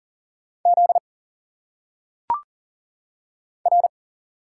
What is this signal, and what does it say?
Morse code, heard as a keyed tone, about 39 wpm, 700 Hz. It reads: Z R
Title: Single Letters - Phonetic Pronunciation 40wpm